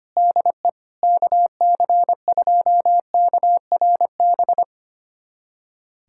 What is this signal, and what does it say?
Morse code, heard as a keyed tone, about 25 wpm, 700 Hz. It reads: DE KC2KRB
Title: USA Callsign DE Callsign - 6 Characters - 25wpm